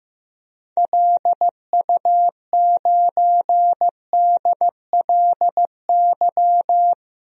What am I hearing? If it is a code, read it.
Morse code, heard as a keyed tone, about 15 wpm, 700 Hz. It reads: LU9DLY
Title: DXCC-100 Call Signs - 6 Characters - 15wpm